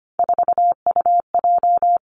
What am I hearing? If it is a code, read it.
Morse code, heard as a keyed tone, about 25 wpm, 700 Hz. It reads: 4UJ
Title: Three Challenging Letter-Number Combinations 25wpm